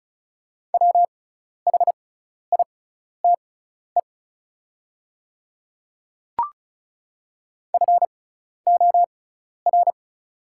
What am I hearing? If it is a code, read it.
Morse code, heard as a keyed tone, about 36 wpm, spaced as for 12 wpm, 690 Hz. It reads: WHITE FOR
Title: 6x Character Spacing - Top 300 Words - 35wpm